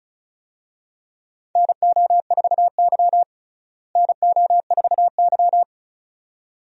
Morse code, heard as a keyed tone, about 36 wpm, 700 Hz. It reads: NO4Y NO4Y
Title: USA Call Signs - 4 Characters - Repeated 2x - 35wpm